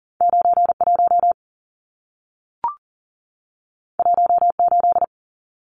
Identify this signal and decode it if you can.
Morse code, heard as a keyed tone, about 40 wpm, 700 Hz. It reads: 91 18
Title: Numbers - 2-Digit 40wpm